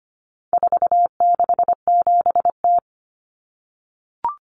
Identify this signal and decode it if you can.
Morse code, heard as a keyed tone, about 25 wpm, 700 Hz. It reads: 467T
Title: Common Cut Numbers - 4-Digit 25wpm